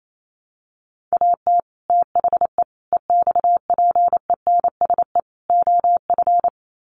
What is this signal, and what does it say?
Morse code, heard as a keyed tone, about 28 wpm, 700 Hz. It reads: AT THE EXPENSE OF
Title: Sets of 4 Words 28wpm